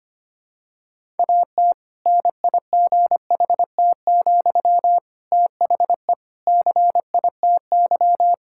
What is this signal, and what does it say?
Morse code, heard as a keyed tone, about 25 wpm, 700 Hz. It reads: AT NIGHT, THE CITY
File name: practice / 25wpm / Difficult Sentences with Hints - 25wpm